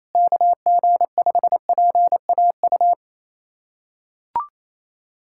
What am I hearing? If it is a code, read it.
Morse code, heard as a keyed tone, about 28 wpm, 700 Hz. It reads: KG5PAU
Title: USA Callsign DE Callsign 28wpm